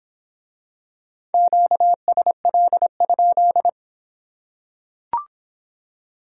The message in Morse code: QSL?

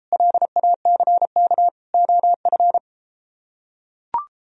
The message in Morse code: LACK OF

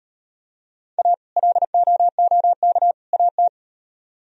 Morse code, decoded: A LOOK AT

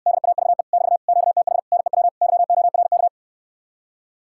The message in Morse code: DAVE <BT> QTH NEW YORK

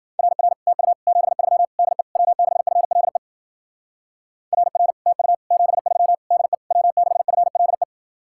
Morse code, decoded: RR TU 73 DE W6FLE RR TU 73 DE W6FLE

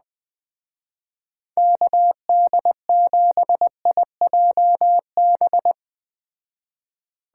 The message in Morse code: KD7IJB

